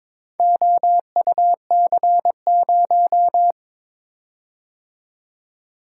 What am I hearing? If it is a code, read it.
Morse code, heard as a keyed tone, about 22 wpm, 700 Hz. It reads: OUC0
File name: practice / 22wpm / Four Letter-Number Combinations - 22wpm